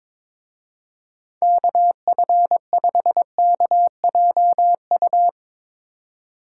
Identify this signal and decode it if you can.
Morse code, heard as a keyed tone, about 24 wpm, 700 Hz. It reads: KF5KJU
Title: Speed Racing - USA 2x3 Call Signs 22wpm